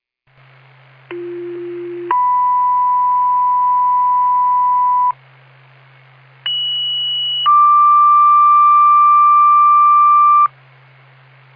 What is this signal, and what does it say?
Two-tone page, heard as a one-second tone, then a three-second tone, about 350 then 990 Hz, about 2950 then 1180 Hz. It is heard from 1.1 to 10.5 s.